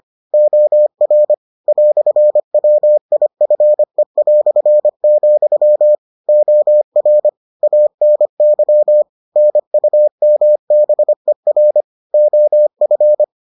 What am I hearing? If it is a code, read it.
Morse code, heard as a keyed tone, about 25 wpm, 600 Hz. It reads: OR "WIFE", OR ANY NUMBER OF